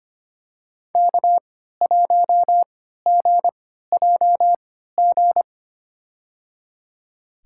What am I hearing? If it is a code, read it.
Morse code, heard as a keyed tone, about 25 wpm, spaced as for 14 wpm, 700 Hz. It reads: K1GJG
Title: USA Call Signs - 5 Characters - 3x Character Spacing - 25wpm